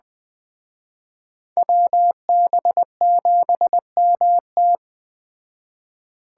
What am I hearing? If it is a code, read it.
Morse code, heard as a keyed tone, about 21 wpm, 700 Hz. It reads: WB7MT